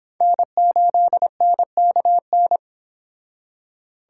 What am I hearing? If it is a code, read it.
Morse code, heard as a keyed tone, about 26 wpm, 700 Hz. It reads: N8NKN